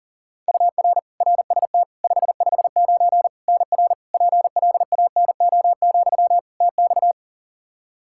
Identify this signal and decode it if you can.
Morse code, heard as a keyed tone, about 37 wpm, 700 Hz. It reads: UR RST 559 NR PLANO, TX
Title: Practice QSO - Exchange 1 - Beginning - 40wpm